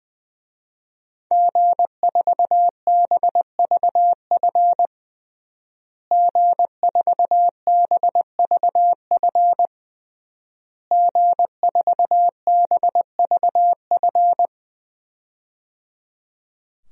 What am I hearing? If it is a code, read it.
Morse code, heard as a keyed tone, about 21 wpm, 700 Hz. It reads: G4BVF G4BVF G4BVF
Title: DXCC-100 Call Signs - 5 Characters - Repeated 3x - 20wpm